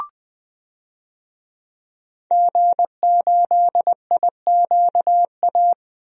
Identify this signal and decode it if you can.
Morse code, heard as a keyed tone, about 20 wpm, 700 Hz. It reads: G8IQA
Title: DXCC-100 Call Signs - 5 Characters - 20wpm